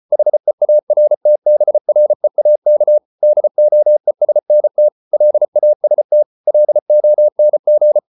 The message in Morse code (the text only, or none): HEARTBREAK DOESNT LAST LONG